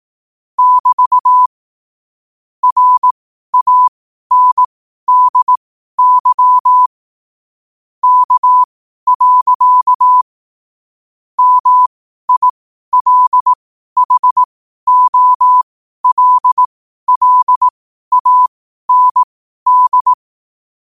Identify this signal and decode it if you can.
Morse code, heard as a keyed tone, about 18 wpm, spaced as for 12 wpm, 1000 Hz. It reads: <BT> RANDY K. MILHOLLAND
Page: Quotes for Sun, 27 Apr 2025 in Morse Code at 12 words per minute.